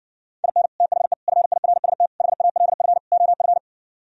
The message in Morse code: IN THE FIRST HALF OF

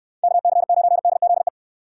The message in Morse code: KC9DZE

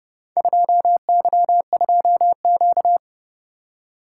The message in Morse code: 2Y2Q

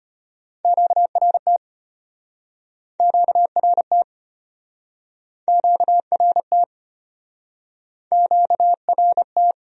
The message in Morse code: QRT QRT QRT QRT

